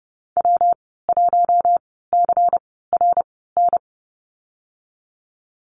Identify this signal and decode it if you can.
Morse code, heard as a keyed tone, about 30 wpm, spaced as for 17 wpm, 700 Hz. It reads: W1CRN
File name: Lesson 19g - Letter w - Call Signs 30wpm